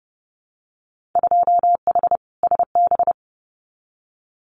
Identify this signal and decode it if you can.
Morse code, heard as a keyed tone, about 30 wpm, 700 Hz. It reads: 2H SB
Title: Winter Field Day Exchanges Encore 30wpm